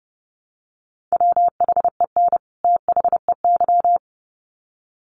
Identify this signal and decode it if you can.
Morse code, heard as a keyed tone, about 28 wpm, 700 Hz. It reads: WHEN THEY
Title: Sets of 2 Words 30wpm